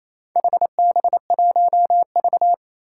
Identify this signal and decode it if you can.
Morse code, heard as a keyed tone, about 29 wpm, 700 Hz. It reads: HB1V
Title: Four Challenging Letter-Number Combinations 28wpm